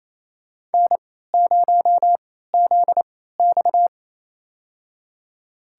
Morse code, heard as a keyed tone, about 28 wpm, spaced as for 16 wpm, 700 Hz. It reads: N0ZX
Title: USA Call Signs - 4 Characters - 3x Character Spacing - 28wpm